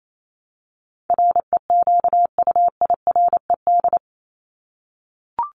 Morse code, heard as a keyed tone, about 28 wpm, 700 Hz. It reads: REQUIRED
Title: ED Suffix - Top 100 Words with Base Words 28wpm